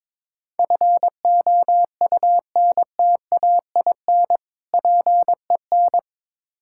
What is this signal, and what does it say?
Morse code, heard as a keyed tone, about 21 wpm, 700 Hz. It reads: FOUNTAIN PEN
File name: Difficult Words with Hints - 22wpm